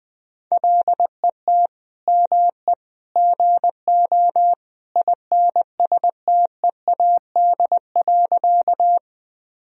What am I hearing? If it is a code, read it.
Morse code, heard as a keyed tone, about 21 wpm, 700 Hz. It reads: LET ME GO INSTEAD.
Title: Sentences from Top 1000 Words - Review Part 4 - 20wpm